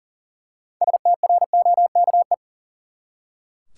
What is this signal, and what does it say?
Morse code, heard as a keyed tone, about 40 wpm, 700 Hz. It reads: STROKE
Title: Six Letter Words - 40wpm